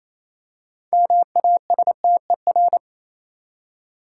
Morse code, heard as a keyed tone, about 28 wpm, 700 Hz. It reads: MASTER
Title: Six Letter Words - 28wpm